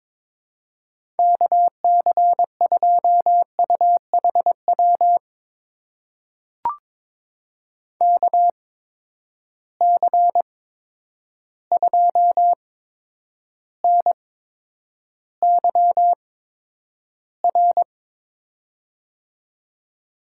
Morse code, heard as a keyed tone, about 23 wpm, spaced as for 10 wpm, 700 Hz. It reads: KC2UHW KC2NYR
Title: USA Call Signs - 8x Character Spacing - 22wpm